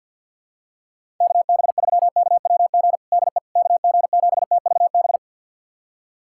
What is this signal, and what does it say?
Morse code, heard as a keyed tone, about 49 wpm, 690 Hz. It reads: KB3KWG DE KG7TVB
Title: USA Callsign DE Callsign 50wpm